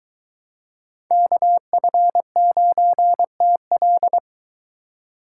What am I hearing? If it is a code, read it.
Morse code, heard as a keyed tone, about 23 wpm, 700 Hz. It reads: KF9TL